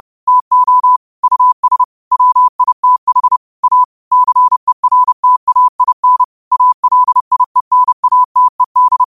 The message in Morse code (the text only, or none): TO US WITH A CERTAIN ALIENATED